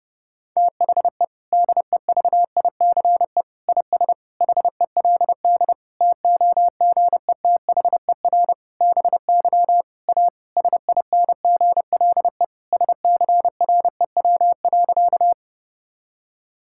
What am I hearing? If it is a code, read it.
Morse code, heard as a keyed tone, about 32 wpm, 700 Hz. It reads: THE DEVICE IS HELD TOGETHER BY A SINGLE SCREW.
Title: Sentences from Top 1600 Words 30wpm